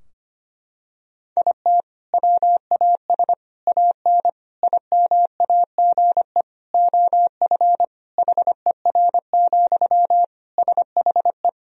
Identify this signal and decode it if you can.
Morse code, heard as a keyed tone, about 25 wpm, 700 Hz. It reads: IT WAS AN IMAGE OF HER, SHE
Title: Sentences from Top 1000 Words - Review Part 4 - 25wpm